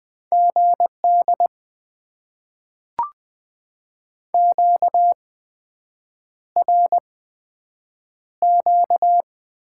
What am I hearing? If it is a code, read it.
Morse code, heard as a keyed tone, about 20 wpm, spaced as for 7 wpm, 700 Hz. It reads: GD QRQ
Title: Top 90 QSO Elements - 8x Character Spacing - 20wpm